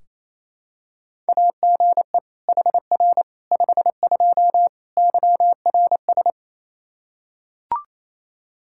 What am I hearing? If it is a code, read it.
Morse code, heard as a keyed tone, about 28 wpm, 700 Hz. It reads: AGE HR 52 YRS
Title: Practice QSO - Exchange 3 - More Info - 28wpm